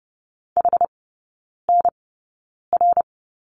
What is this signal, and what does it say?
Morse code, heard as a keyed tone, about 30 wpm, 700 Hz. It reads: H N R
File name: Lesson 19e - Letter w - Three Characters 30wpm